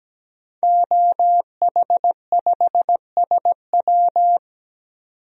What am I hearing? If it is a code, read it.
Morse code, heard as a keyed tone, about 17 wpm, 700 Hz. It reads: OH5SW
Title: DXCC-100 Call Signs - 5 Characters - 17wpm